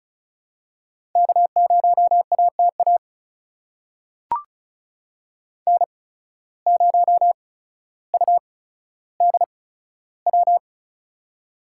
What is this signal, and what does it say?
Morse code, heard as a keyed tone, about 36 wpm, spaced as for 15 wpm, 700 Hz. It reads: K0ATA N0UDW
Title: USA Call Signs - 5 Characters - 8x Character Spacing - 35wpm